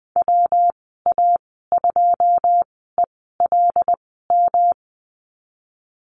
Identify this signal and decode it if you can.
Morse code, heard as a keyed tone, about 20 wpm, spaced as for 15 wpm, 700 Hz. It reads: WA2ELM
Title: 128 Lesson 21g - Number 6 - Call Signs 20wpm